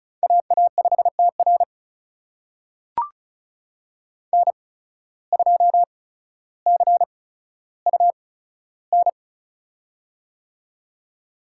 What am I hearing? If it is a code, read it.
Morse code, heard as a keyed tone, about 36 wpm, spaced as for 15 wpm, 700 Hz. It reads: AA5TR N2CUN
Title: USA Call Signs - 5 Characters - 8x Character Spacing - 35wpm